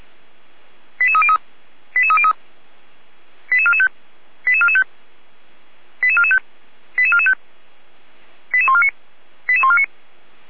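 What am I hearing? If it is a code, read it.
Five-tone selective call (ZVEI), heard as quick runs of five tones, tones of about 70 ms. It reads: 88303 88303 88445 88445 88445 88445 88149 88149
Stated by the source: ZVEI five-tone call 88265, received by SDR.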